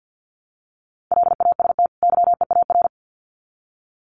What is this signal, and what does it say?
Morse code, heard as a keyed tone, about 50 wpm, 700 Hz. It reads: LAST YEAR
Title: Sets of 2 Words 50wpm